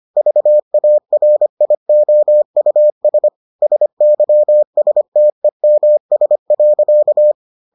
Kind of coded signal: Morse code

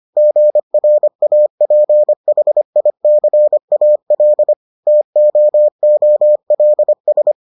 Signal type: Morse code

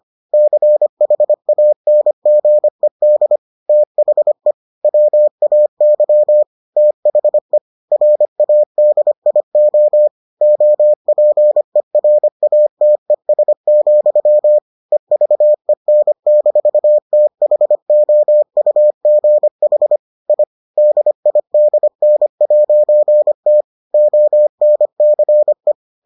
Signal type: Morse code

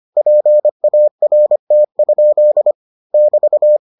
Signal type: Morse code

